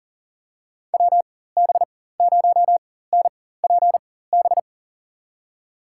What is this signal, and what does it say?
Morse code, heard as a keyed tone, about 38 wpm, spaced as for 18 wpm, 700 Hz. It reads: WB0NPB
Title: USA Call Signs - 6 Characters - 4x Character Spacing - 40wpm